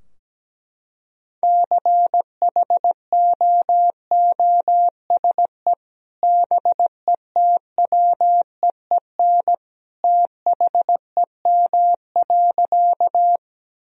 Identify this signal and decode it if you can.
Morse code, heard as a keyed tone, about 17 wpm, 700 Hz. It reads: CHOOSE BETWEEN THEM.
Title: Sentences from Top 1000 Words - Review Part 4 - 17wpm